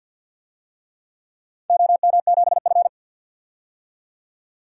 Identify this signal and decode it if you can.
Morse code, heard as a keyed tone, about 49 wpm, 700 Hz. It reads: OM7F
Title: DXCC-100 Call Signs - 4 Characters - 50wpm